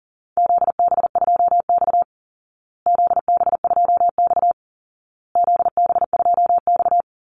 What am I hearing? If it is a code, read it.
Morse code, heard as a keyed tone, about 40 wpm, 700 Hz. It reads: ZB2X ZB2X ZB2X
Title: Call Signs Repeated 2x - 40wpm